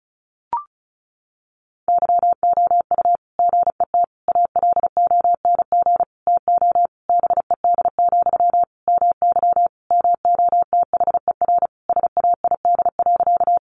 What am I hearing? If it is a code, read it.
Morse code, heard as a keyed tone, about 34 wpm, 700 Hz. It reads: YOU GET ALONG TO BED, MY MOTHER SAID.
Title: Sentences from Top 700 Words 35wpm